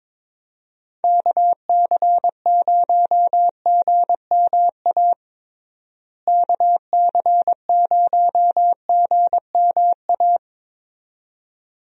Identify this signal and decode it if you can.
Morse code, heard as a keyed tone, about 22 wpm, 700 Hz. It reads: KC0GMA KC0GMA
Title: USA Call Signs - 6 Characters - Repeated 2x - 22wpm